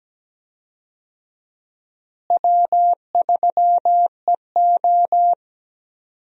Morse code, heard as a keyed tone, about 17 wpm, 700 Hz. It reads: W3EO
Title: USA Call Signs - 4 Characters - Repeated 3x - 17wpm